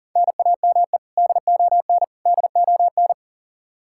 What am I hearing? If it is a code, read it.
Morse code, heard as a keyed tone, about 40 wpm, 700 Hz. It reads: NAME DON DON
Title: Practice QSO - Exchange 1 - Beginning - 40wpm